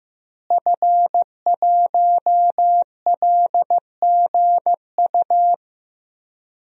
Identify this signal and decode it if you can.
Morse code, heard as a keyed tone, about 15 wpm, 700 Hz. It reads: F1LGU
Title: DXCC-100 Call Signs - 5 Characters - Repeated 2x - 15wpm